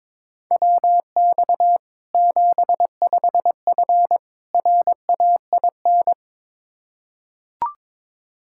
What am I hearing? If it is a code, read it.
Morse code, heard as a keyed tone, about 22 wpm, 700 Hz. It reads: WX 75F RAIN
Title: Practice QSO - Exchange 2 - Info - 22wpm